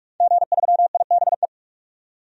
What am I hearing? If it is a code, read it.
Morse code, heard as a keyed tone, about 46 wpm, 700 Hz. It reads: G3IBE
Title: DXCC-100 Call Signs - 5 Characters - 45wpm